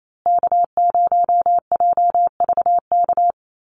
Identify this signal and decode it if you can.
Morse code, heard as a keyed tone, about 28 wpm, 700 Hz. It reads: K0JVK
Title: USA Call Signs Repeated 3x 28wpm